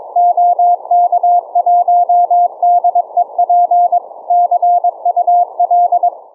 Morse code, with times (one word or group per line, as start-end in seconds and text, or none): 0.2-4.0 OK1DEP
4.3-6.1 CUL